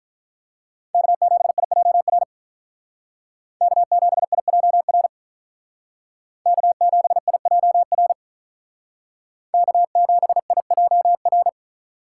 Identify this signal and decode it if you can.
Morse code, heard as a keyed tone, about 43 wpm, 700 Hz. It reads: K7IJR K7IJR K7IJR K7IJR
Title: Speed Racing - USA 1x3 Call Signs 35wpm